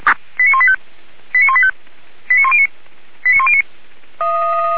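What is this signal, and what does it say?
Five-tone selective call (ZVEI), heard as quick runs of five tones, tones of about 70 ms. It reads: 89185 89185 89109 89109
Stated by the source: ZVEI five-tone selective call received by SDR, from Feuerwehr Orlamünde.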